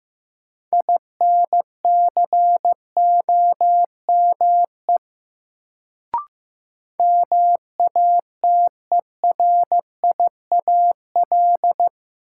Morse code, heard as a keyed tone, about 15 wpm, 700 Hz. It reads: INCOME MATERIAL
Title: Sentences from Top 1000 Words - Review Part 5 - New Word Drill - 15wpm